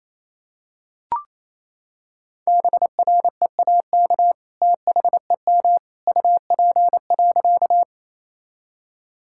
Morse code, heard as a keyed tone, about 28 wpm, 700 Hz. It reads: BREAK THEM UP.
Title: Sentences from Top 1000 Words - Review Part 6 - 28wpm